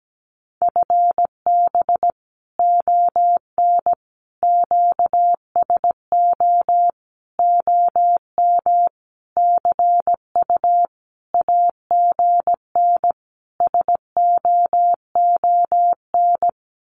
Morse code, heard as a keyed tone, about 17 wpm, 700 Hz. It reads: FB ON QSO OM CU AGN SOON
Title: Common QSO Phrases 17wpm